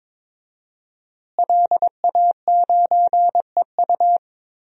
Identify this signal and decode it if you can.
Morse code, heard as a keyed tone, about 21 wpm, 700 Hz. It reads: LA9EU
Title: DXCC-100 Call Signs - 5 Characters - Repeated 2x - 22wpm